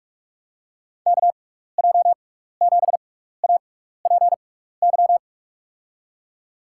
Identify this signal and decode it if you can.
Morse code, heard as a keyed tone, about 46 wpm, spaced as for 15 wpm, 700 Hz. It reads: KJ7APY
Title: USA Call Signs - 6x Character Spacing - 45wpm